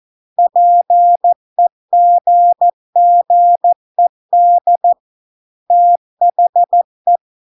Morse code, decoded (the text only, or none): PEGGED THE